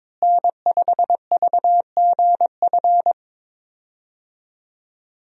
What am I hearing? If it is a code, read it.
Morse code, heard as a keyed tone, about 22 wpm, 700 Hz. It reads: N5VGF
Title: USA Call Signs - 5 Characters - Repeated 3x - 22wpm